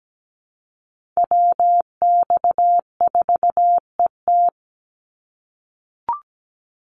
Morse code, heard as a keyed tone, about 17 wpm, 700 Hz. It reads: WX4ET
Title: NA SOTA Chaser Callsigns 17wpm